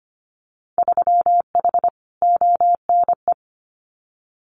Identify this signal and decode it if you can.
Morse code, heard as a keyed tone, about 25 wpm, 700 Hz. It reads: 3H ONE
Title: Winter Field Day Exchanges Encore 25wpm